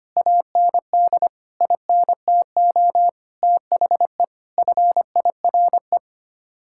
Morse code, heard as a keyed tone, about 25 wpm, 700 Hz. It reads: AND INTO THE FIRE
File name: Sets of 9 Words - 25wpm